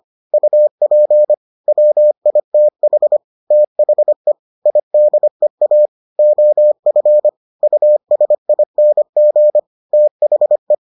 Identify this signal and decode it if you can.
Morse code, heard as a keyed tone, about 25 wpm, 600 Hz. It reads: UP WITH THE IDEA OF USING THE